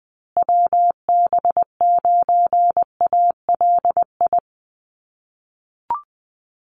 Morse code, heard as a keyed tone, about 20 wpm, 700 Hz. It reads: WB9ALI
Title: USA Call Signs Repeated 3x 20wpm